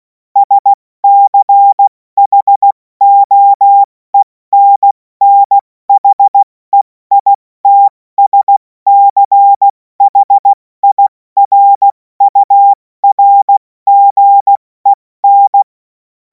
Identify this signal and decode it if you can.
Morse code, heard as a keyed tone, about 16 wpm, 800 Hz. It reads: SCHOENNHEITSCHIRURGEN
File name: Morsekurs / Witze / Witz 11. August 2026 70 F80